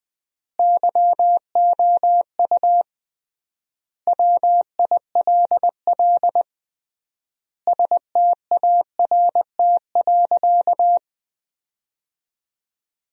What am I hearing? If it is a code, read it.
Morse code, heard as a keyed tone, about 20 wpm, 700 Hz. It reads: YOU WILL START.
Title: 3x Word-Spacing - Sentences from Top 300 Words - 20wpm